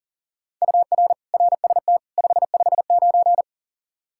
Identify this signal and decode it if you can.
Morse code, heard as a keyed tone, about 40 wpm, 700 Hz. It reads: UR RST 559
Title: Practice QSO - Exchange 1 - Beginning - 40wpm